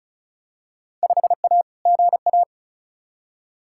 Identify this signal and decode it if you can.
Morse code, heard as a keyed tone, about 36 wpm, 700 Hz. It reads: 5A GA
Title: Field Day Exchanges Encore - 35wpm